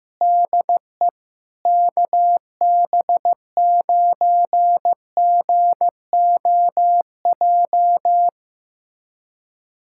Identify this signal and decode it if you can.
Morse code, heard as a keyed tone, about 15 wpm, 700 Hz. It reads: DE KB9GOJ
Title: USA Callsign DE Callsign - 6 Characters - 15wpm